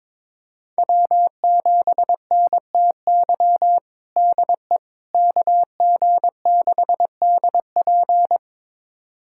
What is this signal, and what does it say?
Morse code, heard as a keyed tone, about 22 wpm, 700 Hz. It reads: W7NTY DE KG6DP
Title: USA Callsign DE Callsign - 5 Characters - 22wpm